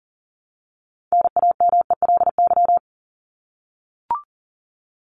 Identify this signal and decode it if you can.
Morse code, heard as a keyed tone, about 40 wpm, 700 Hz. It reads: NAMELY
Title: LY Suffix - Top 100 Words with Base Words 40wpm